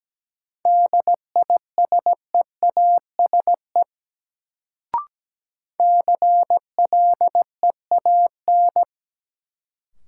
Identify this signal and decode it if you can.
Morse code, heard as a keyed tone, about 17 wpm, 700 Hz. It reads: DISEASE CLEAN
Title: Sentences from Top 1100 Words - New Word Drill Alt - 17wpm